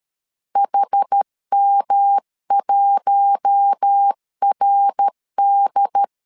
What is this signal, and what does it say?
Morse code, heard as a keyed tone, about 13 wpm, 800 Hz. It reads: HM1RD
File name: LEZIONI / 7 GF9Z / 7 G 12wpm 2055